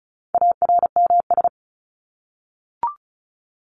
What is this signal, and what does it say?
Morse code, heard as a keyed tone, about 34 wpm, 700 Hz. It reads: ARMS
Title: ES:S Suffix - Top 100 Words with Base Words 35wpm